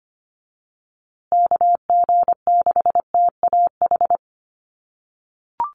Morse code, heard as a keyed tone, about 25 wpm, 700 Hz. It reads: KG6TAH
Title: USA Call Signs Repeated 3x 25wpm